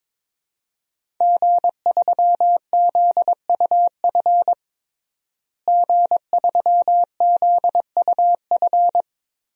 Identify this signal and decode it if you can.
Morse code, heard as a keyed tone, about 22 wpm, 700 Hz. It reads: G3ZUF G3ZUF
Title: DXCC-100 Call Signs - 5 Characters - Repeated 2x - 22wpm